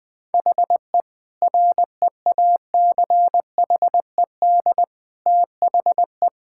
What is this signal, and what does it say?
Morse code, heard as a keyed tone, about 20 wpm, 700 Hz. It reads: HE REACHED THE
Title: Sentences with 3-Word Phrases - 20wpm